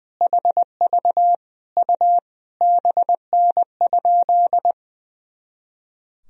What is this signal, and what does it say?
Morse code, heard as a keyed tone, about 19 wpm, 700 Hz. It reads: HV U BN?
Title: Sentences from Top 400 Words and Abbreviations 20wpm